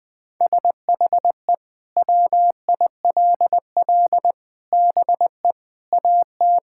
Morse code, heard as a keyed tone, about 20 wpm, 700 Hz. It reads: SHE WILL BE AT
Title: Sentences from Top 1600 Words 20wpm